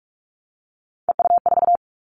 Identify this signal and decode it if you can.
Morse code, heard as a keyed tone, about 44 wpm, 700 Hz. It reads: EU4
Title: Callsign Prefixes for Top 100 DXCC - 45wpm